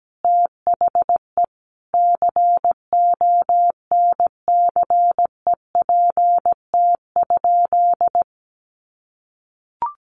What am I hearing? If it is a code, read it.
Morse code, heard as a keyed tone, about 17 wpm, 700 Hz. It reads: THE CONCEPT?
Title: Sentences from Top 1100 Words 17wpm